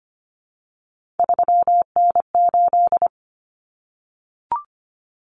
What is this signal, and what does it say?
Morse code, heard as a keyed tone, about 25 wpm, 700 Hz. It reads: 3N8
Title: Common Cut Numbers - 3-Digit 25wpm